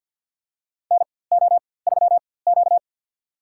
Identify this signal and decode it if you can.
Morse code, heard as a keyed tone, about 47 wpm, spaced as for 23 wpm, 700 Hz. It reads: NO3Q
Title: USA Call Signs - 4x Character Spacing - 50wpm